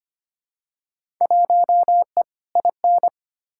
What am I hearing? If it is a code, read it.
Morse code, heard as a keyed tone, about 25 wpm, 700 Hz. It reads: 1E IN
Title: Field Day Exchanges - 25wpm